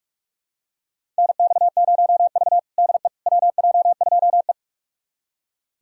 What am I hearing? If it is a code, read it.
Morse code, heard as a keyed tone, about 44 wpm, 700 Hz. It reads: NX0V DE WJ2E
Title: USA Callsign DE Callsign - 4 Characters - 45wpm